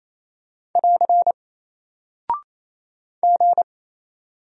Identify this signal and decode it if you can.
Morse code, heard as a keyed tone, about 28 wpm, 700 Hz. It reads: <AR> G
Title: Single Character (Letter:Number:Punctuation:Prosign) Alt 28wpm